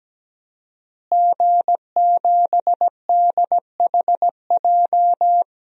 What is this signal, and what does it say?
Morse code, heard as a keyed tone, about 17 wpm, 700 Hz. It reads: G7DHJ
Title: DXCC-100 Call Signs - 5 Characters - Repeated 3x - 17wpm